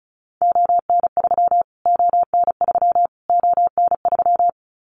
Morse code, heard as a keyed tone, about 34 wpm, 700 Hz. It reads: ON3 ON3 ON3
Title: Callsign Prefixes for Top 25 DXCC - Repeated 3x - 35wpm